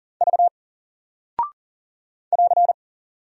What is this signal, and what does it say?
Morse code, heard as a keyed tone, about 39 wpm, 700 Hz. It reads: V <AR>
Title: Single Character (Letter:Number:Punctuation:Prosign) Alt 40wpm